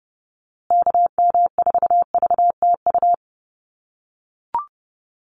Morse code, heard as a keyed tone, about 29 wpm, 700 Hz. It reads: KM4VTU
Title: USA Call Signs Repeated 2x 30wpm